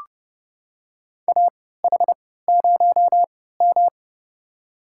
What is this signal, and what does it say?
Morse code, heard as a keyed tone, about 32 wpm, spaced as for 17 wpm, 700 Hz. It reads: AH0M
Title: USA Call Signs - 4 Characters - 3x Character Spacing - 30wpm